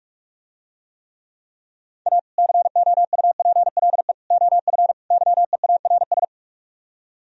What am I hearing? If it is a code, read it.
Morse code, heard as a keyed tone, about 47 wpm, 700 Hz. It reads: A COUPLE OF YEARS